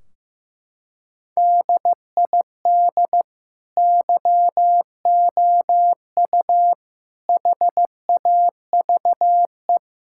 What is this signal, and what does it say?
Morse code, heard as a keyed tone, about 15 wpm, 700 Hz. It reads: DID YOU HAVE
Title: Sentences from Top 1600 Words 15wpm